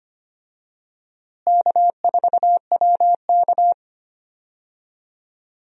Morse code, heard as a keyed tone, about 25 wpm, 700 Hz. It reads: K4WK
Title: USA Call Signs - 4 Characters - Repeated 2x - 25wpm